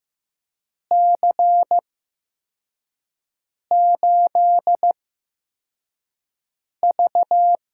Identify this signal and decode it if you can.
Morse code, heard as a keyed tone, about 15 wpm, spaced as for 4 wpm, 700 Hz. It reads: C8V
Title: USA Call Signs - 4 Characters - 8x Character Spacing - 15wpm